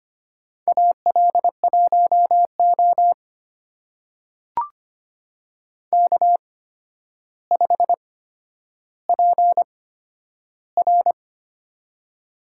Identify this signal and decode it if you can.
Morse code, heard as a keyed tone, about 25 wpm, spaced as for 11 wpm, 700 Hz. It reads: AL1O K5PR
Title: USA Call Signs - 4 Characters - 8x Character Spacing - 25wpm